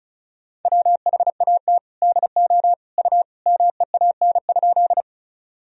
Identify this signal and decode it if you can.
Morse code, heard as a keyed tone, about 35 wpm, 700 Hz. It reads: WHAT DO U MEAN?
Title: Sentences from Top 200 Words and Abbreviations 35wpm